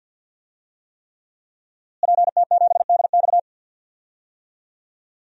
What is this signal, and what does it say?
Morse code, heard as a keyed tone, about 50 wpm, 700 Hz. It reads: PT7DX
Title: DXCC-100 Call Signs - 5 Characters - Repeated 3x - 50wpm